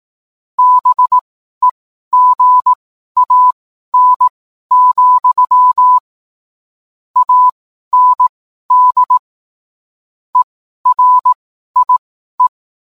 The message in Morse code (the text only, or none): BEGAN, AND ERIE